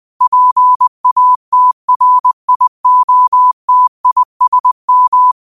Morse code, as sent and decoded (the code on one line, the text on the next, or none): .--. .- - .-. .. --- - .. ... --
PATRIOTISM